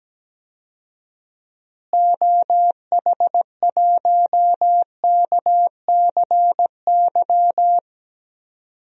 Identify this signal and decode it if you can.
Morse code, heard as a keyed tone, about 17 wpm, 700 Hz. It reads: OH1KCY
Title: DXCC-100 Call Signs - 6 Characters - Repeated 3x - 17wpm